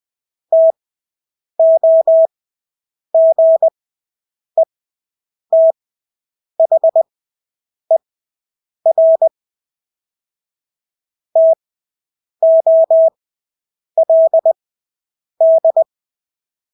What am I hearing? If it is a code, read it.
Morse code, heard as a keyed tone, about 20 wpm, spaced as for 8 wpm, 650 Hz. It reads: TOGETHER TOLD
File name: Engelske ord 20/8 wpm